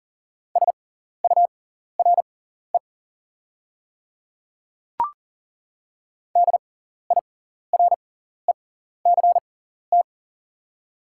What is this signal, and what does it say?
Morse code, heard as a keyed tone, about 40 wpm, spaced as for 14 wpm, 690 Hz. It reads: SURE DIRECT
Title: 6x Character Spacing - Top 500 Words - 40wpm